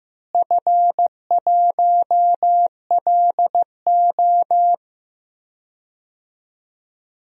Morse code, heard as a keyed tone, about 15 wpm, 700 Hz. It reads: F1LO
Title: DXCC-100 Call Signs - 4 Characters - Repeated 2x - 15wpm